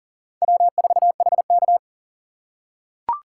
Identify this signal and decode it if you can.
Morse code, heard as a keyed tone, about 40 wpm, 690 Hz. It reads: W4HK
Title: USA Call Signs - 4 Characters - 2x Character Spacing - 40wpm